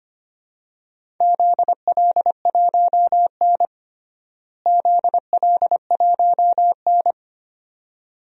Morse code, heard as a keyed tone, about 25 wpm, 700 Hz. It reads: ZL1N ZL1N
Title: DXCC-100 Call Signs - 4 Characters - Repeated 2x - 25wpm